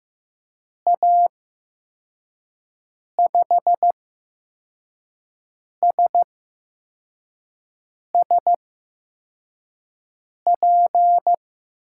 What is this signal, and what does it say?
Morse code, heard as a keyed tone, about 15 wpm, spaced as for 4 wpm, 700 Hz. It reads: A5SSP
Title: USA Call Signs - 6 Characters - 8x Character Spacing - 15wpm